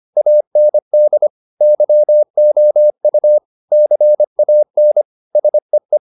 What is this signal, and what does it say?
Morse code, heard as a keyed tone, about 25 wpm, 600 Hz. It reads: AND YOU CAN SEE